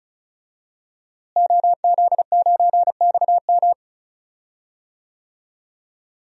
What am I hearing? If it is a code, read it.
Morse code, heard as a keyed tone, about 34 wpm, 700 Hz. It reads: OZ9XM
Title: DXCC-100 Call Signs - 5 Characters - Repeated 2x - 35wpm